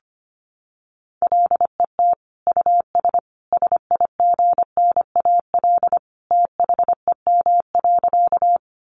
Morse code, heard as a keyed tone, about 25 wpm, 700 Hz. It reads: LET US SIGNAL THEM.
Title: Sentences from Top 1400 Words 25wpm